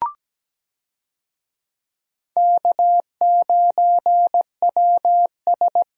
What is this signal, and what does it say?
Morse code, heard as a keyed tone, about 17 wpm, 700 Hz. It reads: K9WS
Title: USA Call Signs - 4 Characters - 17wpm